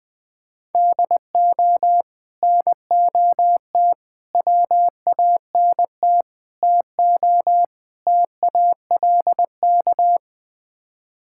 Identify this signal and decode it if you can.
Morse code, heard as a keyed tone, about 20 wpm, 700 Hz. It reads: DO NOT WANT TO TALK
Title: Build A Phrase - 5-Word Phrases - 20wpm